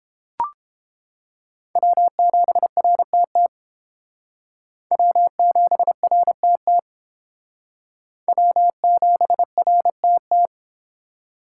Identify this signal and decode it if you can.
Morse code, heard as a keyed tone, about 29 wpm, 700 Hz. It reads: W7RTT W7RTT W7RTT